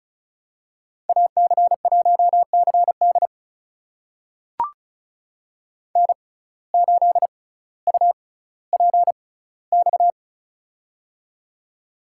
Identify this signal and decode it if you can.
Morse code, heard as a keyed tone, about 34 wpm, spaced as for 18 wpm, 700 Hz. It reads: AC1CD N8UPX
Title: USA Call Signs - 5 Characters - 6x Character Spacing - 35wpm